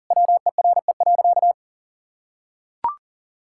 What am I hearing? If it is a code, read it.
Morse code, heard as a keyed tone, about 40 wpm, 700 Hz. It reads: WERE.
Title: Sentences from Top 1000 Words - Review Part 6 - 40wpm